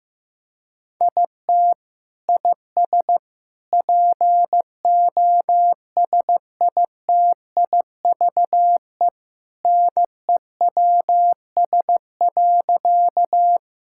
Morse code, heard as a keyed tone, about 16 wpm, 700 Hz. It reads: IT IS POSITIVE NEWS.
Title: Sentences from Top 1000 Words - Review Part 5 - 15wpm